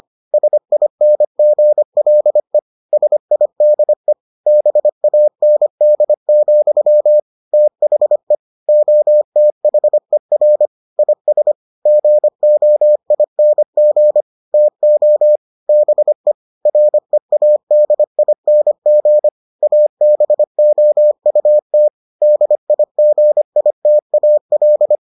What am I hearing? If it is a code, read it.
Morse code, heard as a keyed tone, about 25 wpm, 600 Hz. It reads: SINGLE SIDE BAND, THE OTHER IS GOING TO BE READING ABOUT DIGITAL